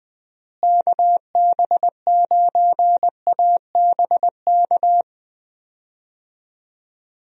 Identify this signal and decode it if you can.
Morse code, heard as a keyed tone, about 20 wpm, 700 Hz. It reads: KB9ABK
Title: USA Call Signs - 6 Characters - Repeated 3x - 20wpm